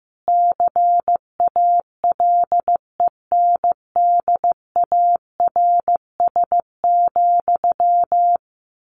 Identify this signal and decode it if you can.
Morse code, heard as a keyed tone, about 15 wpm, 700 Hz. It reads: CALENDARS,
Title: A Sound Of Thunder 15wpm